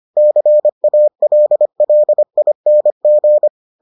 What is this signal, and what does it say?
Morse code, heard as a keyed tone, about 25 wpm, 600 Hz. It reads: CALLING